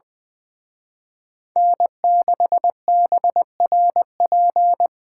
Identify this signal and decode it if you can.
Morse code, heard as a keyed tone, about 25 wpm, 700 Hz. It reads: N6BRP
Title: Speed Racing - USA 1x3 Call Signs 20wpm